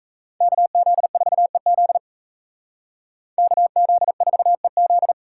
Morse code, heard as a keyed tone, about 36 wpm, 700 Hz. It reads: KZ4EZ KZ4EZ